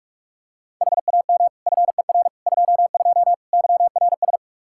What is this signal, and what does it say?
Morse code, heard as a keyed tone, about 46 wpm, 700 Hz. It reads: HAM FER 22 YRS
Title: K1USN SST QSO Practice 45wpm